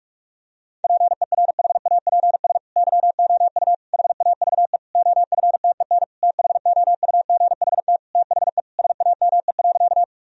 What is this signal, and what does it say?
Morse code, heard as a keyed tone, about 46 wpm, 700 Hz. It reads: PERHAPS YOU HAVE OFTEN THOUGHT THE SAME.
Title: Sentences from Top 1000 Words - Review Part 1 - 45wpm